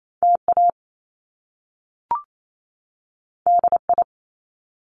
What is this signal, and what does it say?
Morse code, heard as a keyed tone, about 28 wpm, 700 Hz. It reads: TA DI
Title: Top Two Letter Combinations in Words 28wpm